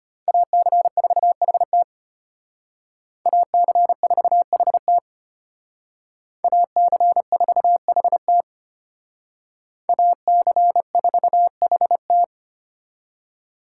Morse code, heard as a keyed tone, about 32 wpm, 700 Hz. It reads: AC4HT AC4HT AC4HT AC4HT